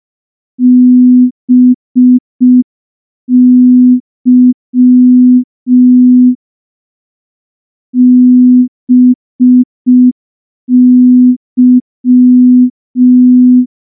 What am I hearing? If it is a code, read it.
Morse code, heard as a keyed tone, about 5 wpm, 250 Hz. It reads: BY BY
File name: morse / morse-training-05WPM